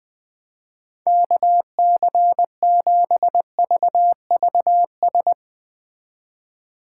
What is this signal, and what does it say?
Morse code, heard as a keyed tone, about 20 wpm, 700 Hz. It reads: KC7VVS
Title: Speed Racing - USA 2x3 Call Signs 17wpm